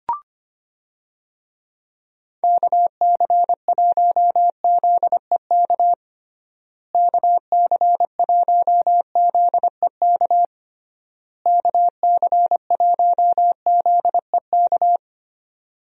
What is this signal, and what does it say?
Morse code, heard as a keyed tone, about 25 wpm, 700 Hz. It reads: KC1ZEK KC1ZEK KC1ZEK
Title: USA Call Signs - 6 Characters - Repeated 3x - 25wpm